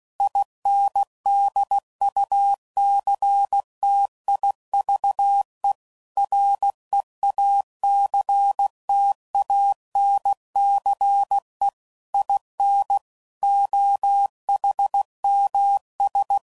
Morse code, heard as a keyed tone, about 16 wpm, 800 Hz. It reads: INDUCTIVE REACTANCE IN OHMS
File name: CW Code Groups @15WPM / Technical Subjects / XL